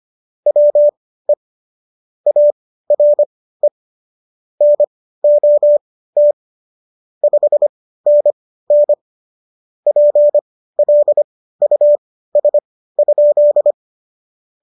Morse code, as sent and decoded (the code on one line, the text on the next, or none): .-- . / .- .-. / -. --- - / ..... -. -. / .--. .-.. ..- ... ..--..
WE ARE NOT 5NN PLUS?